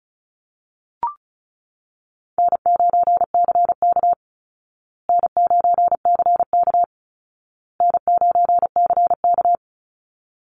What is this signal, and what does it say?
Morse code, heard as a keyed tone, about 34 wpm, 700 Hz. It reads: N9CK N9CK N9CK
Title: Easy Call Signs Repeated 3x - 35wpm